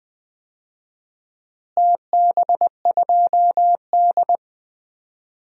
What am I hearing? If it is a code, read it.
Morse code, heard as a keyed tone, about 20 wpm, 700 Hz. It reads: TB2D
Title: DXCC-100 Call Signs - 4 Characters - Repeated 2x - 20wpm